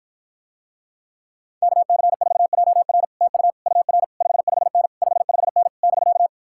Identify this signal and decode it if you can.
Morse code, heard as a keyed tone, about 52 wpm, 700 Hz. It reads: KC4JR TU UR 55N 55N <BK>